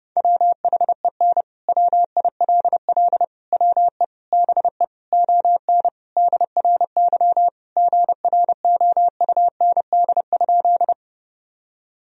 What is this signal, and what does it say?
Morse code, heard as a keyed tone, about 32 wpm, 700 Hz. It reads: WHEN WILL WE BE ON DRY GROUND?
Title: Sentences from Top 1000 Words - Review Part 3 - 30wpm